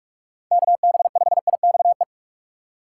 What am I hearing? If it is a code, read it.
Morse code, heard as a keyed tone, about 43 wpm, 700 Hz. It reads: KB5IXE
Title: USA Call Signs - 6 Characters - 4x Character Spacing - 45wpm